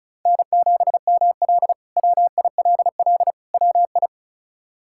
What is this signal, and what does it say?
Morse code, heard as a keyed tone, about 36 wpm, 700 Hz. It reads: N7ML WILL WI
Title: K1USN SST QSO Practice - 35wpm